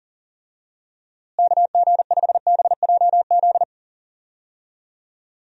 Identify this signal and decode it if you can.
Morse code, heard as a keyed tone, about 40 wpm, 700 Hz. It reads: KG5BJZ
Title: USA Call Signs - 6 Characters - Repeated 2x - 40wpm